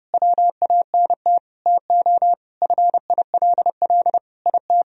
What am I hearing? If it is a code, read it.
Morse code, heard as a keyed tone, about 30 wpm, 700 Hz. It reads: WANT TO FILL IT
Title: Sentences from Top 1000 Words - Review Part 3 - 30wpm